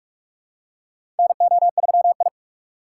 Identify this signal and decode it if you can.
Morse code, heard as a keyed tone, about 43 wpm, 700 Hz. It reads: NO3I
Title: USA Call Signs - 2x Character Spacing - 45wpm